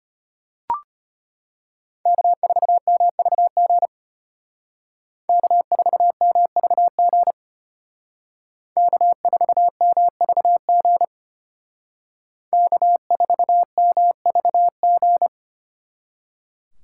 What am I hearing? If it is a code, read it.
Morse code, heard as a keyed tone, about 33 wpm, 700 Hz. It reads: K4MVG K4MVG K4MVG K4MVG